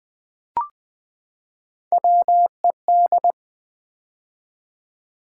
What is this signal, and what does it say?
Morse code, heard as a keyed tone, about 20 wpm, 700 Hz. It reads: WED
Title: Build A Word - 3 Letters At A Time - 9-Letter Words - 20wpm